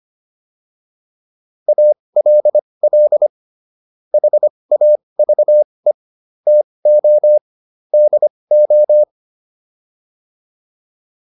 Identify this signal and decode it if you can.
Morse code, heard as a keyed tone, about 25 wpm, spaced as for 20 wpm, 600 Hz. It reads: ALL HAVE TO DO